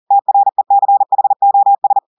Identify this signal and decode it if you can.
Morse code, heard as a keyed tone, about 40 wpm, 800 Hz. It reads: TRECHOS
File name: audios / portugais / portugais texte en clair 1 vitesse 40 mots minutes